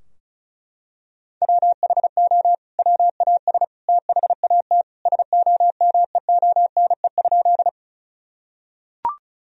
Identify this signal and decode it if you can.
Morse code, heard as a keyed tone, about 34 wpm, 700 Hz. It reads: WHO WAS THAT SOMEONE?
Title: Sentences from Top 1000 Words - Review Part 4 - 35wpm